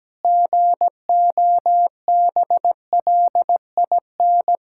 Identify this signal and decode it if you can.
Morse code, heard as a keyed tone, about 17 wpm, 700 Hz. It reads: GOBLIN
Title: Pangram Challenge v2 - 17wpm